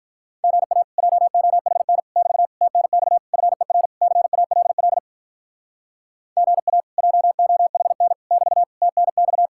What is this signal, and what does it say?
Morse code, heard as a keyed tone, about 47 wpm, 700 Hz. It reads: GA JOHN <BT> TNX FER CALL GA JOHN <BT> TNX